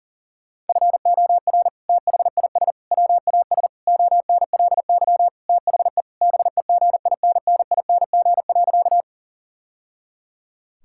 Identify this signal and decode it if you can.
Morse code, heard as a keyed tone, about 40 wpm, 690 Hz. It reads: FOR THIS WAS ONLY THE BEGINNING.
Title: Sentences from Top 1000 Words - Review Part 5 - 40wpm